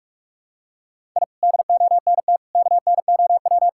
Morse code, heard as a keyed tone, about 47 wpm, 700 Hz. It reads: I DONT KNOW
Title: Sentences from Top 1000 Words - Review Part 6 - 45wpm